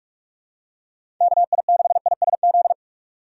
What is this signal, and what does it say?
Morse code, heard as a keyed tone, about 43 wpm, 700 Hz. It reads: KI6ISZ
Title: USA Call Signs - 6 Characters - 3x Character Spacing - 45wpm